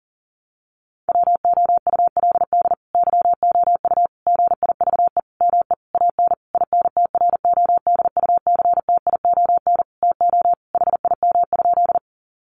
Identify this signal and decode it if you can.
Morse code, heard as a keyed tone, about 40 wpm, 700 Hz. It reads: WOULD YOU GIVE ME AN INTRODUCTION TO HIM?
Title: Sentences from Top 1000 Words 40wpm